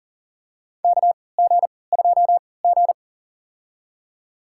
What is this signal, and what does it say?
Morse code, heard as a keyed tone, about 40 wpm, spaced as for 23 wpm, 700 Hz. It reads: KG2G
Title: USA Call Signs - 4 Characters - 3x Character Spacing - 40wpm